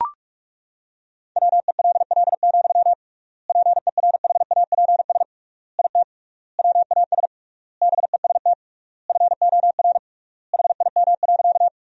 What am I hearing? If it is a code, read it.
Morse code, heard as a keyed tone, about 41 wpm, 700 Hz. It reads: WELL, PERHAPS IT WAS BEST FOR HIM.
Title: 3x Word-Spacing - Sentences from Top 500 Words - 45wpm